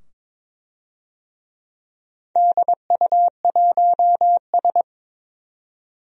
Morse code, heard as a keyed tone, about 22 wpm, 700 Hz. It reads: DU1S
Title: DXCC-100 Call Signs - 4 Characters - 22wpm